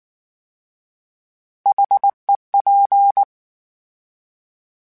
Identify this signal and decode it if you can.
Morse code, heard as a keyed tone, about 19 wpm, 800 Hz. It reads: HEP
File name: H Prefixes / H mp3s